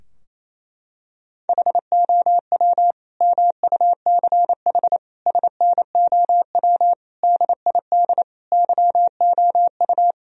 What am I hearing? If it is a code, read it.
Morse code, heard as a keyed tone, about 28 wpm, 700 Hz. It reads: HOW MUCH SNOW DID YOU
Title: Sentences from Top 1000 Words - Review Part 3 - 28wpm